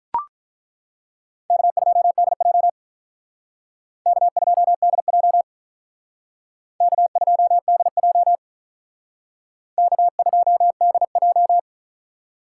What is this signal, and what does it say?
Morse code, heard as a keyed tone, about 43 wpm, 700 Hz. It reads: K2DJ K2DJ K2DJ K2DJ
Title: Speed Racing - USA 1x2 Call Signs 35wpm